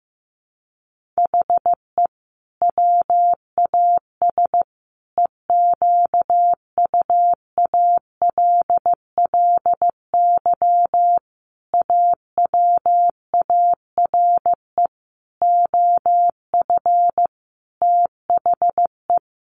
Morse code, heard as a keyed tone, about 15 wpm, 700 Hz. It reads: HE WAS EQUALLY AWARE OF THE
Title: Sentences from Top 1400 Words 15wpm